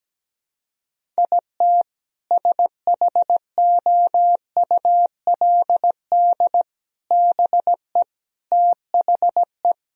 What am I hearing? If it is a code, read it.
Morse code, heard as a keyed tone, about 17 wpm, 700 Hz. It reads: IT SHOULD BE THE
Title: Sentences from Top 1000 Words - Review Part 1 - 17wpm